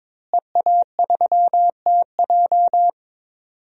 Morse code, heard as a keyed tone, about 22 wpm, 700 Hz. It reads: EA3TJ
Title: DXCC-100 Call Signs - 5 Characters - Repeated 3x - 22wpm